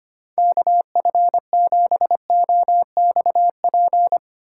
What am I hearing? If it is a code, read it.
Morse code, heard as a keyed tone, about 25 wpm, 700 Hz. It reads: KF7OXP
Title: USA Call Signs - 6 Characters - 4x Character Spacing - 25wpm